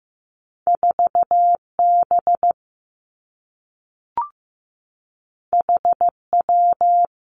Morse code, heard as a keyed tone, about 15 wpm, 700 Hz. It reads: 4 B H W
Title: Two Challenging Letter-Number Combinations 15wpm